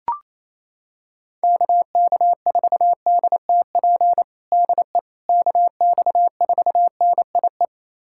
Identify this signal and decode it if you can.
Morse code, heard as a keyed tone, about 28 wpm, 700 Hz. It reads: KK4DTP DE KX4NIE
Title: USA Callsign DE Callsign - 6 Characters - 28wpm